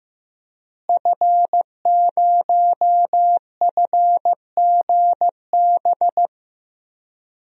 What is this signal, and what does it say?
Morse code, heard as a keyed tone, about 15 wpm, 700 Hz. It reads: F0FGB
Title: DXCC-100 Call Signs - 5 Characters - Repeated 3x - 15wpm